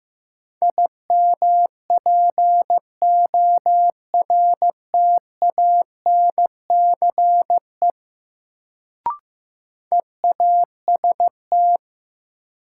Morse code, heard as a keyed tone, about 15 wpm, 700 Hz. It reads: IMPORTANCE EAST
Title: Sentences from Top 1200 Words - New Word Drill Alt - 15wpm